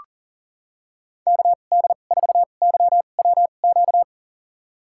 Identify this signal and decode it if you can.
Morse code, heard as a keyed tone, about 40 wpm, spaced as for 29 wpm, 700 Hz. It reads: KD4YWQ
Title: USA Call Signs - 6 Characters - 2x Character Spacing - 40wpm